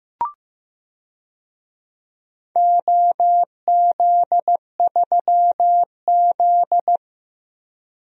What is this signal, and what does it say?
Morse code, heard as a keyed tone, about 15 wpm, 700 Hz. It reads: OZ3Z
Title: DXCC-100 Call Signs - 4 Characters - 15wpm